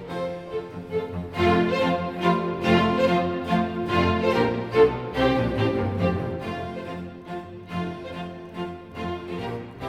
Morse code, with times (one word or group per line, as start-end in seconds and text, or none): none